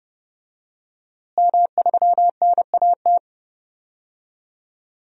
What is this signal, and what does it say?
Morse code, heard as a keyed tone, about 30 wpm, 700 Hz. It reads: M3NAT
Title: DXCC-100 Call Signs - 5 Characters - Repeated 2x - 30wpm